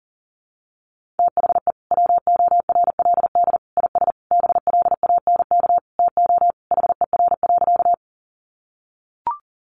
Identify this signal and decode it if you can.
Morse code, heard as a keyed tone, about 40 wpm, 700 Hz. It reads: THE WORLD IS BLANK TO HER.
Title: Sentences from Top 1500 Words 40wpm